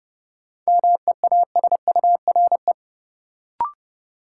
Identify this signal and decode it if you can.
Morse code, heard as a keyed tone, about 30 wpm, 700 Hz. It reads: MEASURE
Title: Sentences from Top 1000 Words - Review Part 1 - New Word Drill - 30wpm